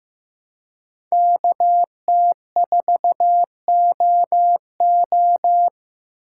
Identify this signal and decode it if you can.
Morse code, heard as a keyed tone, about 15 wpm, 700 Hz. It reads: KT4OO
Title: USA Call Signs - 3x Character Spacing - 15wpm